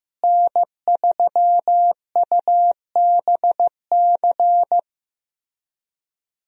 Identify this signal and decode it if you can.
Morse code, heard as a keyed tone, about 15 wpm, 700 Hz. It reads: N3UBC
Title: USA Call Signs - 5 Characters - Repeated 3x - 15wpm